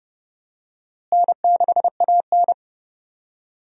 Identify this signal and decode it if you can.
Morse code, heard as a keyed tone, about 32 wpm, 700 Hz. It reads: N6AN
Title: USA Call Signs - 4 Characters - Repeated 2x - 30wpm